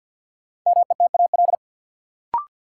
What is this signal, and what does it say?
Morse code, heard as a keyed tone, about 53 wpm, 700 Hz. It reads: METAL
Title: Sentences from Top 1000 Words - Review Part 6 - New Word Drill - 50wpm